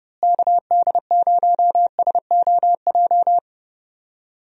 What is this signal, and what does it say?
Morse code, heard as a keyed tone, about 31 wpm, 700 Hz. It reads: KD0SOJ
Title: USA Call Signs - 6 Characters - Repeated 2x - 30wpm